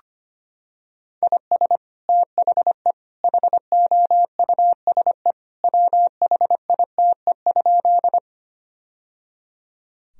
Morse code, heard as a keyed tone, about 25 wpm, 700 Hz. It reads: IS THE HOUSE WHITE?
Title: Sentences from Top 1000 Words - Review Part 1 - 25wpm